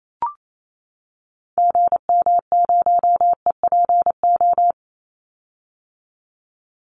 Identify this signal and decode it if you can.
Morse code, heard as a keyed tone, about 28 wpm, 700 Hz. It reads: GM0EPO
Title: Call Signs - 28wpm